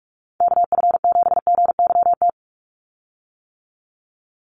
Morse code, heard as a keyed tone, about 44 wpm, 700 Hz. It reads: KF7GYT
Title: USA Call Signs 45wpm